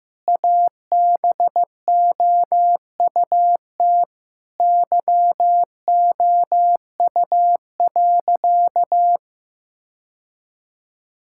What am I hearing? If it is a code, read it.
Morse code, heard as a keyed tone, about 15 wpm, 700 Hz. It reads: ABOUT YOU.
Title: Sentences from Top 1000 Words - Review Part 6 - 15wpm